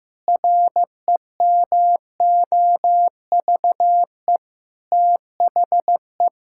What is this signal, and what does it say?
Morse code, heard as a keyed tone, about 15 wpm, 700 Hz. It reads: REMOVE THE
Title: Sentences from Top 1000 Words - Review Part 6 - 15wpm